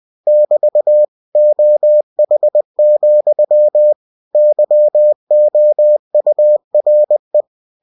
Morse code, heard as a keyed tone, about 20 wpm, 600 Hz. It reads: <BT> OH, YOURE